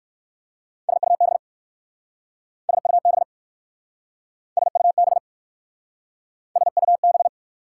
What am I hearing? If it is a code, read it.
Morse code, heard as a keyed tone, about 59 wpm, 700 Hz. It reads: SUB SUB SUB SUB